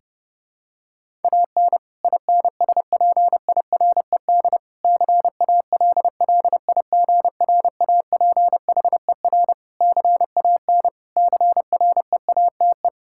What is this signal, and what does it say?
Morse code, heard as a keyed tone, about 30 wpm, 700 Hz. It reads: AN INSPIRED CALLIGRAPHER CAN CREATE
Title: Pangram Challenge v2 - 30wpm